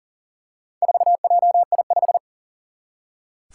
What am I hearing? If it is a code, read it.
Morse code, heard as a keyed tone, about 40 wpm, 700 Hz. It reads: 4JI5
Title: Four Challenging Letter-Number Combinations 40wpm